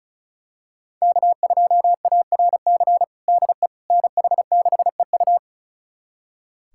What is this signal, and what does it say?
Morse code, heard as a keyed tone, about 34 wpm, 690 Hz. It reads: K2ARC DE NH6EU
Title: USA Callsign DE Callsign - 5 Characters - 35wpm